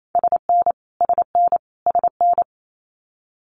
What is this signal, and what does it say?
Morse code, heard as a keyed tone, about 28 wpm, 700 Hz. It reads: SN SN SN
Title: Abbreviations Repeated 3x - 28wpm